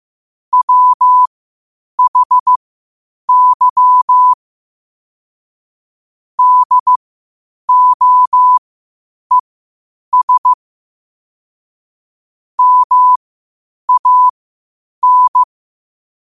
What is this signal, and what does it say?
Morse code, heard as a keyed tone, about 15 wpm, spaced as for 8 wpm, 1000 Hz. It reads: WHY DOES MAN
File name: qotdfiles / 8wpm